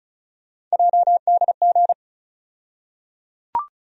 Morse code, heard as a keyed tone, about 36 wpm, 700 Hz. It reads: JDG
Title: USA Call Sign Postfixes - 3 Letters 35wpm